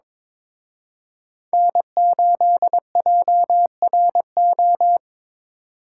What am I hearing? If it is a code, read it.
Morse code, heard as a keyed tone, about 26 wpm, 700 Hz. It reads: N8JRO
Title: Speed Racing - USA 1x3 Call Signs 22wpm